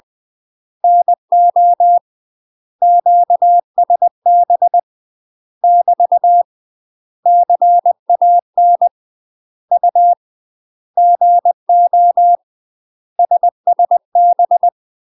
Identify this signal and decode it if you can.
Morse code, heard as a keyed tone, about 20 wpm, 700 Hz. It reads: NO QSB <BT> CAN U GO SSB
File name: UK MASTER QSOs TRAINING / 20WPM / UK MASTER QSOs 0034